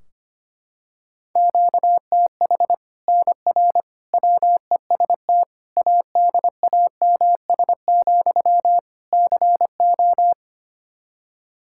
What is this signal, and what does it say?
Morse code, heard as a keyed tone, about 25 wpm, 700 Hz. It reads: QTH NR WEST ADAMS, CO
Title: Practice QSO - Exchange 1 - Beginning - 25wpm